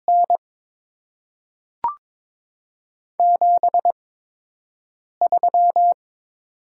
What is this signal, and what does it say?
Morse code, heard as a keyed tone, about 22 wpm, spaced as for 6 wpm, 700 Hz. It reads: N 73
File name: Top 20 QSO Elements - 8x Character Spacing - 22wpm